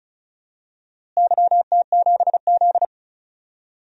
Morse code, heard as a keyed tone, about 36 wpm, 700 Hz. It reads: YT7Z
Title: DXCC-100 Call Signs - 4 Characters - Repeated 2x - 35wpm